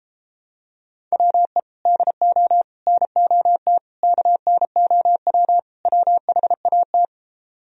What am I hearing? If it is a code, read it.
Morse code, heard as a keyed tone, about 33 wpm, 700 Hz. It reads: WE DO NOT KNOW WHAT